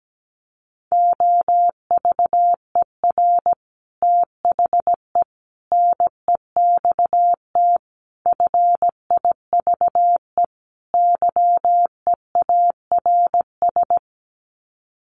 Morse code, heard as a keyed tone, about 17 wpm, 700 Hz. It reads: OVER THE NEXT FIVE YEARS
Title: Sets of 5 Words 17wpm